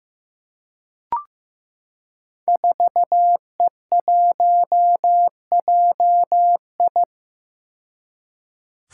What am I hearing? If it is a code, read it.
Morse code, heard as a keyed tone, about 15 wpm, 700 Hz. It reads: 4E1JI
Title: Five Challenging Letter-Number Combinations 15wpm